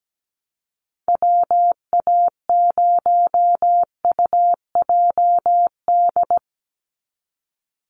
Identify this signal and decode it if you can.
Morse code, heard as a keyed tone, about 17 wpm, 700 Hz. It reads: WA0UJD
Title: USA Call Signs 17wpm